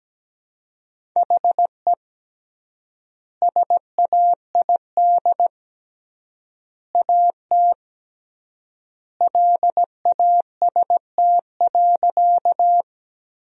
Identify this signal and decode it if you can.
Morse code, heard as a keyed tone, about 17 wpm, 700 Hz. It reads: HE SAID AT LAST.
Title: 3x Word-Spacing - Sentences from Top 300 Words - 17wpm